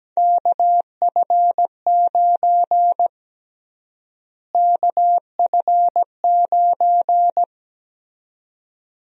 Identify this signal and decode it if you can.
Morse code, heard as a keyed tone, about 17 wpm, 700 Hz. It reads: KF9 KF9
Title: USA Call Sign Prefixes - Repeated 2x - 17wpm